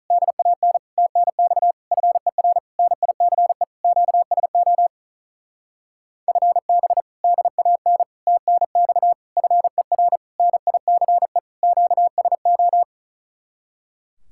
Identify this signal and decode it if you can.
Morse code, heard as a keyed tone, about 44 wpm, 700 Hz. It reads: DAN TNX FER NICE QSO FB DAN TNX FER NICE QSO
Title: Speed-Racing Common QSO Phrases - 35wpm